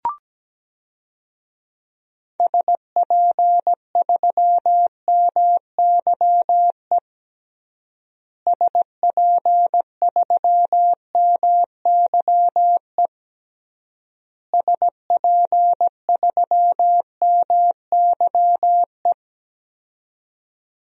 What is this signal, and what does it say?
Morse code, heard as a keyed tone, about 17 wpm, 700 Hz. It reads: SP3MYE SP3MYE SP3MYE
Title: DXCC-100 Call Signs - 6 Characters - Repeated 3x - 17wpm